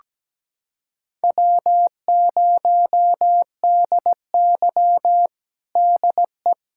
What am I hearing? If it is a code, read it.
Morse code, heard as a keyed tone, about 17 wpm, 700 Hz. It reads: W0DY DE
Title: USA Callsign DE Callsign - 4 Characters - 17wpm